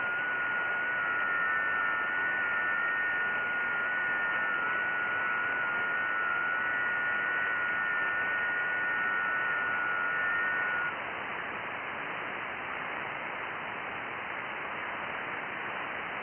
FT8.
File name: IC-7100